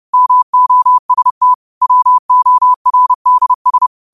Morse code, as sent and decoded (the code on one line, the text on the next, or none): -- --- ... - / .-- --- .-. -.. ...
MOST WORDS